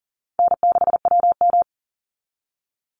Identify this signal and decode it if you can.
Morse code, heard as a keyed tone, about 40 wpm, 700 Hz. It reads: N6WM
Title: Call Signs Repeated 3x - 40wpm